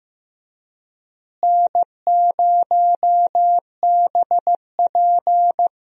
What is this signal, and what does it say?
Morse code, heard as a keyed tone, about 15 wpm, 700 Hz. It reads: N0BP
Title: USA Call Signs - 4 Characters - Repeated 3x - 15wpm